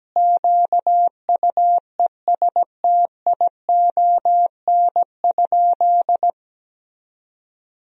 Morse code, heard as a keyed tone, about 17 wpm, 700 Hz. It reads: QUESTION?
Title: Sentences from Top 400 Words and Abbreviations 17wpm